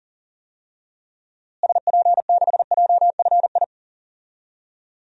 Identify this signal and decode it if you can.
Morse code, heard as a keyed tone, about 37 wpm, 700 Hz. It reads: SP6JFI
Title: DXCC-100 Call Signs - 6 Characters - Repeated 2x - 40wpm